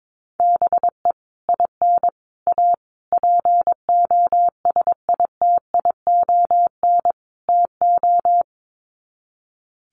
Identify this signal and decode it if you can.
Morse code, heard as a keyed tone, about 22 wpm, 700 Hz. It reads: BE IN A POSITION TO
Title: Sets of 5 Words Encore2 22wpm